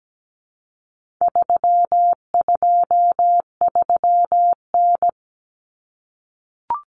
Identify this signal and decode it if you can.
Morse code, heard as a keyed tone, about 17 wpm, 700 Hz. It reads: 323N
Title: Common Cut Numbers - 4-Digit 17wpm